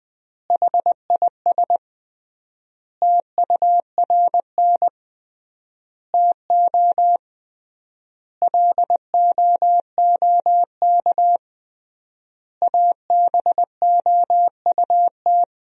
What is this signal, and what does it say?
Morse code, heard as a keyed tone, about 21 wpm, 700 Hz. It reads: HIS TURN TO LOOK ABOUT
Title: 3x Word-Spacing - Sentences from Top 200 Words - 20wpm